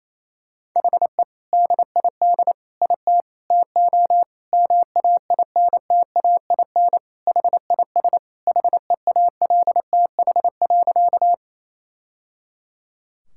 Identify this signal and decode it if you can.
Morse code, heard as a keyed tone, about 28 wpm, 700 Hz. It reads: HE DID IT TO MAINTAIN HIS HEALTH.
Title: Sentences from Top 1000 Words - Review Part 5 - 28wpm